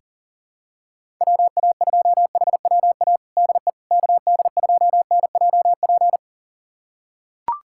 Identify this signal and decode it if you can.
Morse code, heard as a keyed tone, about 37 wpm, 700 Hz. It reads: WA2HWA DE KD2NJP
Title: USA Callsign DE Callsign - 6 Characters - 40wpm